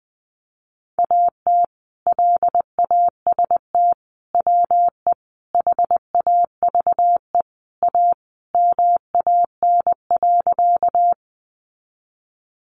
Morse code, heard as a keyed tone, about 20 wpm, 700 Hz. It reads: AT LAST WE HAVE A MAN.
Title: Sentences from Top 300 Words 20wpm